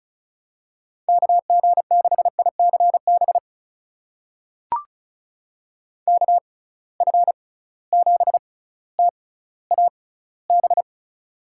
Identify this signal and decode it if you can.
Morse code, heard as a keyed tone, about 36 wpm, spaced as for 18 wpm, 700 Hz. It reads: KG6ICB KF7TAB
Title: USA Call Signs - 6 Characters - 6x Character Spacing - 35wpm